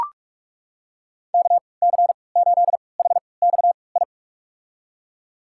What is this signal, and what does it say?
Morse code, heard as a keyed tone, about 43 wpm, spaced as for 25 wpm, 700 Hz. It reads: KC8HXI
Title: USA Call Signs - 6 Characters - 3x Character Spacing - 45wpm